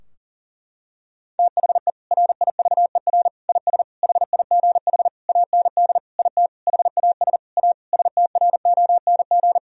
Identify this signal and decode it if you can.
Morse code, heard as a keyed tone, about 40 wpm, 700 Hz. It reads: THE RIVER IS HIGH AND IT HAS A STRONG
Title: Sentences from Top 1000 Words - Review Part 4 - 40wpm